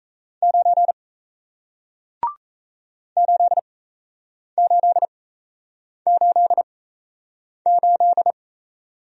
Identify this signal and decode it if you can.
Morse code, heard as a keyed tone, about 38 wpm, spaced as for 8 wpm, 700 Hz. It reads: 9 8888